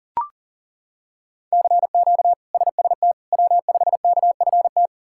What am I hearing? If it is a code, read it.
Morse code, heard as a keyed tone, about 40 wpm, 700 Hz. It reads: CQ SST W5KFT
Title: K1USN SST QSO Practice - 40wpm